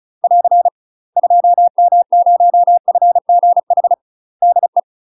Morse code, heard as a keyed tone, about 36 wpm, 700 Hz. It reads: <AR> 2M0FGH DE
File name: UK MASTER QSOs TRAINING / 35WPM / UK MASTER QSOs 0002